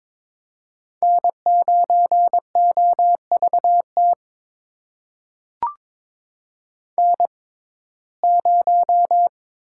Morse code, heard as a keyed tone, about 22 wpm, spaced as for 16 wpm, 700 Hz. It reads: N9OVT N0
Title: USA Call Signs - 5 Characters - 6x Character Spacing - 22wpm